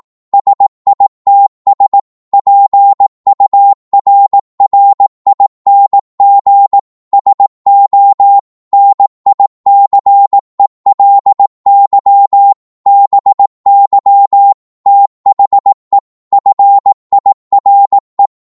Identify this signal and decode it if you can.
Morse code, heard as a keyed tone, about 18 wpm, 800 Hz. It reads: SITS PURRING SO NICELY BY THE FIRE
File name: Morse / psg-18wpm